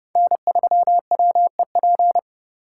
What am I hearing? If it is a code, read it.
Morse code, heard as a keyed tone, about 30 wpm, 700 Hz. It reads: N3WEP